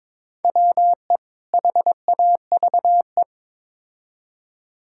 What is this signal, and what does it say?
Morse code, heard as a keyed tone, about 22 wpm, 700 Hz. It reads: WE HAVE
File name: practice / 22wpm / Build A Phrase - 5-Word Phrases - 22wpm